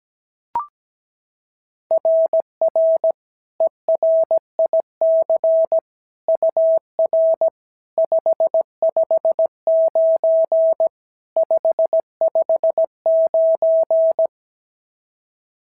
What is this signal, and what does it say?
Morse code, heard as a keyed tone, about 17 wpm, 650 Hz. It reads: RR ERIC UR 559 559
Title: SOTA QSO Practice 17wpm